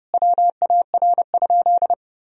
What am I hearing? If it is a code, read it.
Morse code, heard as a keyed tone, about 30 wpm, 700 Hz. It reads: WAR?
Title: Sentences from Top 1000 Words - Review Part 4 - 30wpm